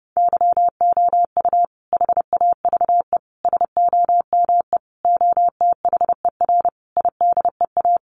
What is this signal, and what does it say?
Morse code, heard as a keyed tone, about 30 wpm, 700 Hz. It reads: YOU HAVE SOME OTHER IDEA
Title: Sentences from Top 1300 Words 30wpm